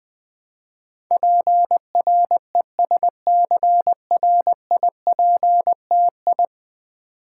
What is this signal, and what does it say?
Morse code, heard as a keyed tone, about 20 wpm, 700 Hz. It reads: PRESCRIPTI
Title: Build A Word - 5 Letters At A Time - 12-Letter Words - 20wpm